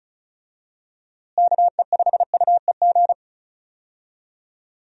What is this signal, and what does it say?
Morse code, heard as a keyed tone, about 36 wpm, 700 Hz. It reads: KE5UEG
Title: USA Call Signs - 6 Characters - Repeated 2x - 35wpm